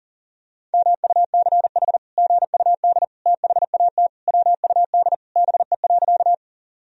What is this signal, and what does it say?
Morse code, heard as a keyed tone, about 40 wpm, 700 Hz. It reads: MUCH GUD THAT WUD BE.
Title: Sentences from Top 200 Words and Abbreviations 40wpm